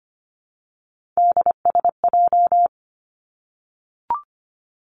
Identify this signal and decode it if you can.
Morse code, heard as a keyed tone, about 25 wpm, 700 Hz. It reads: DSJ
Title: Three Challenging Letter-Number Combinations 25wpm